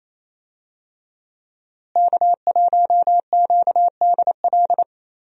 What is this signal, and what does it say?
Morse code, heard as a keyed tone, about 28 wpm, 700 Hz. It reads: K1QDL
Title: USA Call Signs - 5 Characters - Repeated 3x - 28wpm